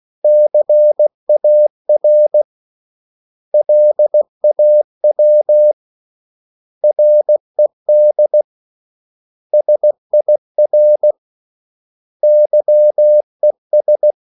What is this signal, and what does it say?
Morse code, heard as a keyed tone, about 16 wpm, 600 Hz. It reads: CAR LAW RED SIR YES
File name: CW / Practice / 500MostCommonEnglishWords16